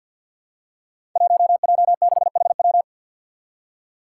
Morse code, heard as a keyed tone, about 47 wpm, 700 Hz. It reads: 1J6HW
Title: Five Challenging Letter-Number Combinations 50wpm